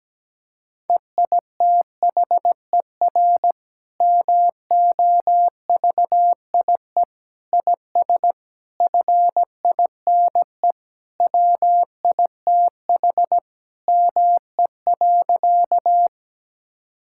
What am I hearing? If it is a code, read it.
Morse code, heard as a keyed tone, about 17 wpm, 700 Hz. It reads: EITHER MOVIE IS FINE WITH ME.
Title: Sentences from Top 1000 Words - Review Part 5 - 17wpm